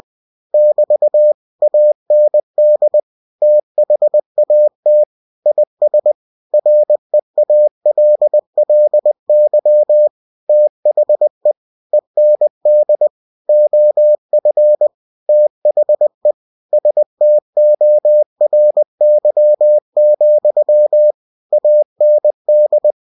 Morse code, heard as a keyed tone, about 20 wpm, 600 Hz. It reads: <BT> AND THAT IS REALLY THE END OF THE STORY, AND
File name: cw-books / winnie-the-pooh / winnie-20wpm-page-0101